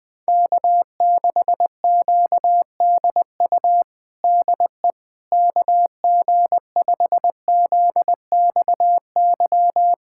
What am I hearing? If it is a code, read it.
Morse code, heard as a keyed tone, about 21 wpm, 700 Hz. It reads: K6QDU DE KG5ZXY
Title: USA Callsign DE Callsign 20wpm